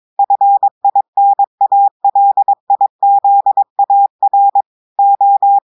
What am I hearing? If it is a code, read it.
Morse code, heard as a keyed tone, about 22 wpm, 800 Hz. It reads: FINALIZAR O
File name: audios / portugais / portugais texte en clair 2 vitesse 22 mots minutes